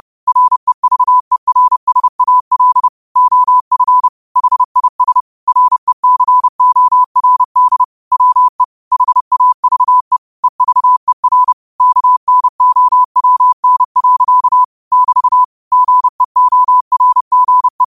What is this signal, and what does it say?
Morse code, heard as a keyed tone, about 30 wpm, 1000 Hz. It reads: REVERSAL OF HIS RECORD WE HAVE EVER KNOWN. <BT> GEORGE
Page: 30 WPM Morse code quotes